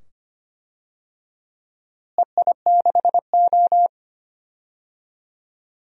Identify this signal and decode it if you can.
Morse code, heard as a keyed tone, about 25 wpm, 700 Hz. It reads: EI6O
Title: DXCC-100 Call Signs - 4 Characters - Repeated 2x - 25wpm